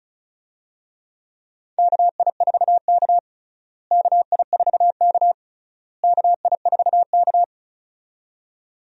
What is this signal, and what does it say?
Morse code, heard as a keyed tone, about 34 wpm, 700 Hz. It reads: KI4K KI4K KI4K
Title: USA Call Signs - 4 Characters - Repeated 3x - 35wpm